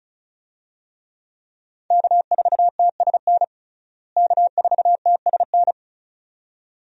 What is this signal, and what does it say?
Morse code, heard as a keyed tone, about 36 wpm, 700 Hz. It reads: K4TSN K4TSN
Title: USA Call Signs - 5 Characters - Repeated 2x - 35wpm